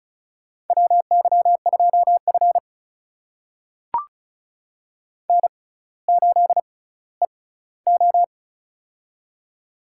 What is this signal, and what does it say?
Morse code, heard as a keyed tone, about 36 wpm, spaced as for 18 wpm, 700 Hz. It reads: WY2F N8EO
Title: USA Call Signs - 4 Characters - 6x Character Spacing - 35wpm